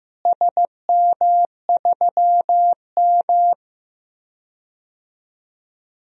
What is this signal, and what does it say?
Morse code, heard as a keyed tone, about 15 wpm, 700 Hz. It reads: SM3M
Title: DXCC-100 Call Signs - 4 Characters - Repeated 2x - 15wpm